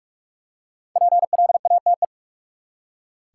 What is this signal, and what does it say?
Morse code, heard as a keyed tone, about 45 wpm, 700 Hz. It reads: PLATE
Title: Five Letter Words - 45wpm